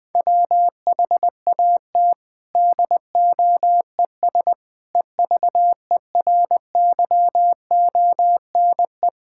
Morse code, heard as a keyed tone, about 20 wpm, 700 Hz. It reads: WHAT DOES EVERYONE
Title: Sentences from Top 1000 Words - Review Part 5 - 20wpm